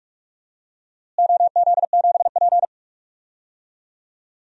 Morse code, heard as a keyed tone, about 42 wpm, 700 Hz. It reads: OZ7P
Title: DXCC-100 Call Signs - 4 Characters - Repeated 2x - 45wpm